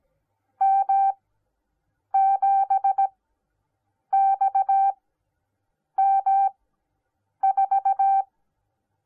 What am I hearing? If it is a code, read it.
Morse code, heard as a keyed tone, about 17 wpm, spaced as for 7 wpm, 780 Hz. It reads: M7XM4